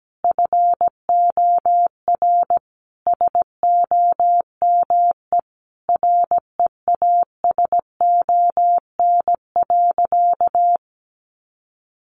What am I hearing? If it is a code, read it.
Morse code, heard as a keyed tone, about 17 wpm, 700 Hz. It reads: FOR SOME REASON.
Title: Sentences from Top 700 Words 17wpm